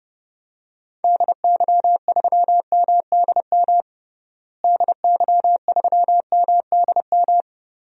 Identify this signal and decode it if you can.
Morse code, heard as a keyed tone, about 32 wpm, 700 Hz. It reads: DY3MDM DY3MDM
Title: DXCC-100 Call Signs - 6 Characters - Repeated 2x - 30wpm